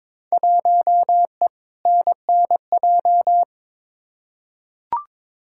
Morse code, heard as a keyed tone, about 21 wpm, 700 Hz. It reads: 1E NNJ
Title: Field Day Exchanges - 22wpm